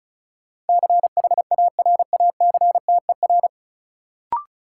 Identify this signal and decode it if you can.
Morse code, heard as a keyed tone, about 34 wpm, 700 Hz. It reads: CHARACTER
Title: Sentences from Top 1200 Words - New Word Drill Alt - 35wpm